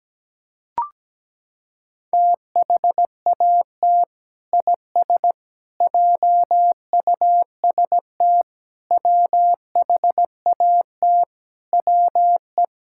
Morse code, heard as a keyed tone, about 17 wpm, 700 Hz. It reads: THAT IS JUST WHAT WE
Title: Sentences from Top 1600 Words 17wpm